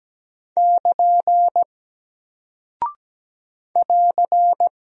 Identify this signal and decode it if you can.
Morse code, heard as a keyed tone, about 17 wpm, 700 Hz. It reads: <KN> <AR>
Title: Single Character (Letter:Number:Punctuation:Prosign) Alt 17wpm